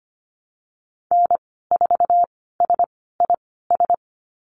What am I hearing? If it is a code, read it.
Morse code, heard as a keyed tone, about 25 wpm, spaced as for 16 wpm, 690 Hz. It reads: N4SIS
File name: Lesson 15g - Number 5 - Call Signs 25wpm